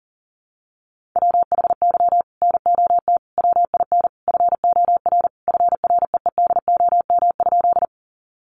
Morse code, heard as a keyed tone, about 40 wpm, 700 Hz. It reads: WHY NOT WIN FOR FREEDOM?
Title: Sentences from Top 1500 Words 40wpm